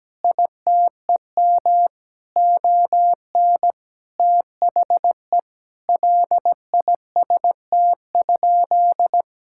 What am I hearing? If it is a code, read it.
Morse code, heard as a keyed tone, about 17 wpm, 700 Hz. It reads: ITEM ON THE LIST?
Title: Sentences from Top 1000 Words - Review Part 6 - 17wpm